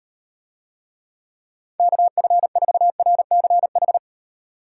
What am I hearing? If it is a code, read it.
Morse code, heard as a keyed tone, about 38 wpm, 700 Hz. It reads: KF4RCH